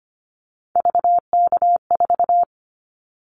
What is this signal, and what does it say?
Morse code, heard as a keyed tone, about 25 wpm, 700 Hz. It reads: VK4
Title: Callsign Prefixes for Top 100 DXCC - 25wpm